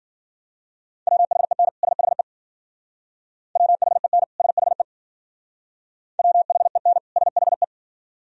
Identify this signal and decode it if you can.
Morse code, heard as a keyed tone, about 52 wpm, 700 Hz. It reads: WHEN SHE WHEN SHE WHEN SHE